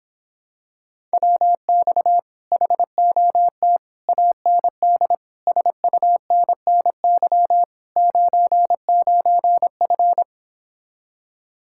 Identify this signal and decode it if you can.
Morse code, heard as a keyed tone, about 26 wpm, 700 Hz. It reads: WX HOT AND SUNNY 99F